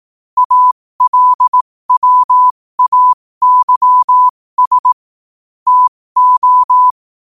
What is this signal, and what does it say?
Morse code, heard as a keyed tone, about 18 wpm, spaced as for 15 wpm, 1000 Hz. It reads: ALWAYS TO
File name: qotdfiles / 15wpm